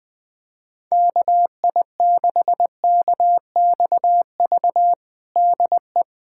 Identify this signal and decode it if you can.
Morse code, heard as a keyed tone, about 20 wpm, 700 Hz. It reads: KI6KXV DE
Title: USA Callsign DE Callsign 20wpm